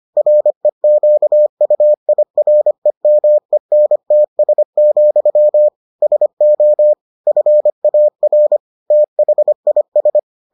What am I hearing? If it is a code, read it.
Morse code, heard as a keyed tone, about 25 wpm, 600 Hz. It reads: REQUIREMENTS, SO FAR THIS